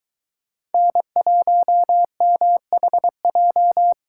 Morse code, heard as a keyed tone, about 22 wpm, 700 Hz. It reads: N1MHJ